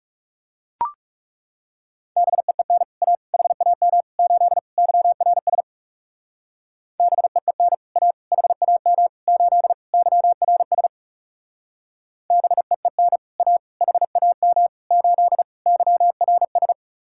Morse code, heard as a keyed tone, about 34 wpm, 700 Hz. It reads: BEEN A HAM 8 YRS BEEN A HAM 8 YRS BEEN A HAM 8 YRS
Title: Speed-Racing Common QSO Phrases - 30wpm